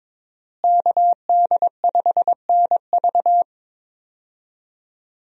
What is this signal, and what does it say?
Morse code, heard as a keyed tone, about 22 wpm, 700 Hz. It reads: KD5NV
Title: USA Call Signs - 5 Characters - Repeated 3x - 22wpm